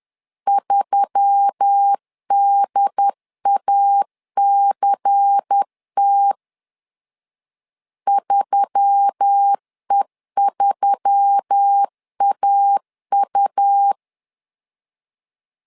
Morse code, heard as a keyed tone, about 10 wpm, 800 Hz. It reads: 3DACT 3E3AU
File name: LEZIONI / 5 LBVC / 5 LBVC 10wpm 2349